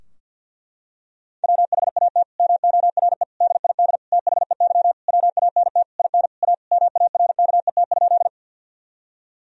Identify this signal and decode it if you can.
Morse code, heard as a keyed tone, about 52 wpm, 700 Hz. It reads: WHAT MORE DID THEY WANT IN A MARKET?
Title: Sentences from Top 1000 Words - Review Part 3 - 50wpm